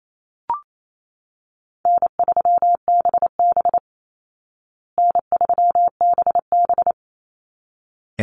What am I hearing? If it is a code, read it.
Morse code, heard as a keyed tone, about 28 wpm, 700 Hz. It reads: N3BB N3BB
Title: Easy Call Signs Repeated 2x - 28wpm